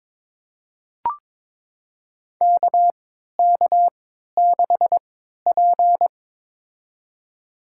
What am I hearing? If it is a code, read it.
Morse code, heard as a keyed tone, about 23 wpm, spaced as for 13 wpm, 700 Hz. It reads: KK6P
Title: USA Call Signs - 4 Characters - 3x Character Spacing - 22wpm